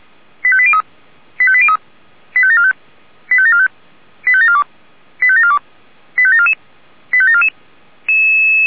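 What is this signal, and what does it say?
Five-tone selective call (ZVEI), heard as quick runs of five tones, tones of about 70 ms. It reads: 85803 85803 85745 85745 85742 85742 85744 85744
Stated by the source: Five-tone ZVEI selective call received on SDR.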